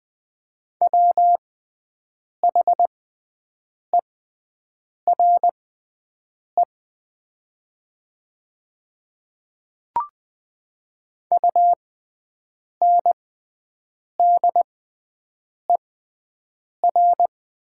Morse code, heard as a keyed tone, about 20 wpm, spaced as for 7 wpm, 700 Hz. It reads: WHERE UNDER
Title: 6x Character Spacing - Top 500 Words - 20wpm